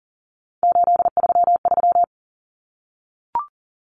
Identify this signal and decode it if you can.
Morse code, heard as a keyed tone, about 40 wpm, 700 Hz. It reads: 833
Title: Common Cut Numbers - 3-Digit 40wpm